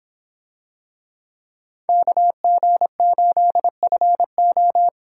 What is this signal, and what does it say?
Morse code, heard as a keyed tone, about 25 wpm, 700 Hz. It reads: KG8FO